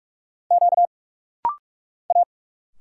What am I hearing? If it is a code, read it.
Morse code, heard as a keyed tone, about 46 wpm, 690 Hz. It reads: Q A
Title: Single Letters - Rapid-Fire 45wpm